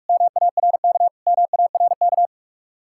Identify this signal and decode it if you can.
Morse code, heard as a keyed tone, about 44 wpm, 700 Hz. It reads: MARK MARK